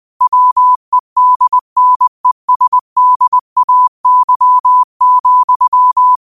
Morse code, heard as a keyed tone, about 20 wpm, 1000 Hz. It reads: WEDNESDAY,
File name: qotdfiles / 20wpm